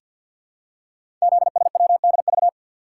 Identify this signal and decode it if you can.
Morse code, heard as a keyed tone, about 53 wpm, 700 Hz. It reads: ZSWDV
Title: Code Groups - 5 Letters - 50wpm